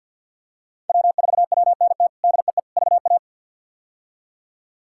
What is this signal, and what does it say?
Morse code, heard as a keyed tone, about 49 wpm, 700 Hz. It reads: W4WNT DEE VA
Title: K1USN SST QSO Practice - 50wpm